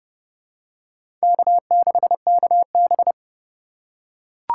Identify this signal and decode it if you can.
Morse code, heard as a keyed tone, about 32 wpm, 700 Hz. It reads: K6KB
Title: USA Call Signs - 4 Characters - 8x Character Spacing - 30wpm